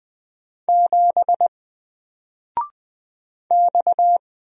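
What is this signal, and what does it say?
Morse code, heard as a keyed tone, about 20 wpm, 700 Hz. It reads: 7 X
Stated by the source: Single letter and number Morse drill at 20 wpm, from N4LCD.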